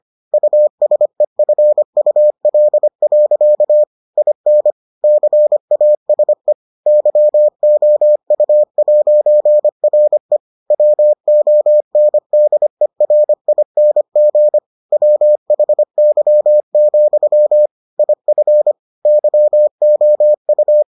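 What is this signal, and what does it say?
Morse code, heard as a keyed tone, about 25 wpm, 600 Hz. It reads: USEFUL. IN CASE YOU'RE WONDERING WHY, IF YOU
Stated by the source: Morse edition of the Foundations of Amateur Radio episodes.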